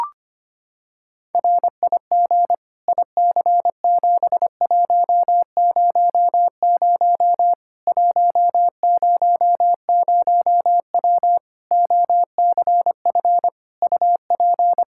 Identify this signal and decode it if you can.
Morse code, heard as a keyed tone, about 25 wpm, 700 Hz. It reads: RIG IC7100 100W OCF UP
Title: Practice QSO - Exchange 2 - Info - 25wpm